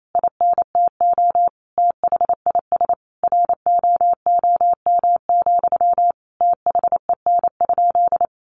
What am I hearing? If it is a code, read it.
Morse code, heard as a keyed tone, about 28 wpm, 700 Hz. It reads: INTO THIS ROOM, THEN?
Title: Sentences from Top 600 Words 28wpm